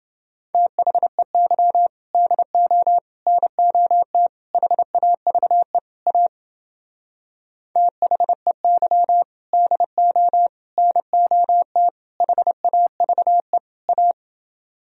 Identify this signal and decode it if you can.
Morse code, heard as a keyed tone, about 29 wpm, 700 Hz. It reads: THEY DO NOT HAVE A THEY DO NOT HAVE A